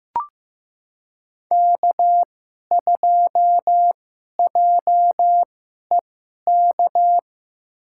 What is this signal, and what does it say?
Morse code, heard as a keyed tone, about 15 wpm, spaced as for 11 wpm, 700 Hz. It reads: K2JEK
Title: USA Call Signs - 5 Characters - 2x Character Spacing - 15wpm